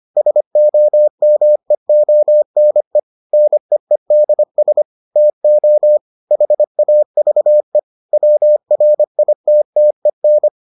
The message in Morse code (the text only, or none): SOMEONE NEEDS TO HAVE WRITTEN